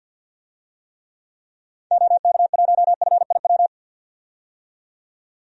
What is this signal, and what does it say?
Morse code, heard as a keyed tone, about 45 wpm, 700 Hz. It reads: OK1FIW
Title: DXCC-100 Call Signs - 6 Characters - 50wpm